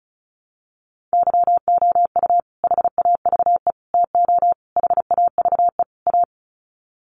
Morse code, heard as a keyed tone, about 34 wpm, 700 Hz. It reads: YOU HAVE TO HAVE A
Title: Sets of 5 Words Encore 35wpm